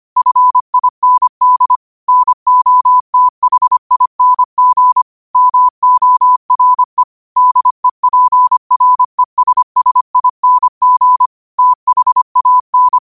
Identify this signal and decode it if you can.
Morse code, heard as a keyed tone, about 25 wpm, 1000 Hz. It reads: FIND NOTHING MORE DEPRESSING THAN
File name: qotdfiles / 25wpm